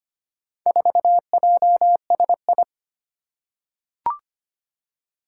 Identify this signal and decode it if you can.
Morse code, heard as a keyed tone, about 25 wpm, 700 Hz. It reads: 4JSI
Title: Four Challenging Letter-Number Combinations 25wpm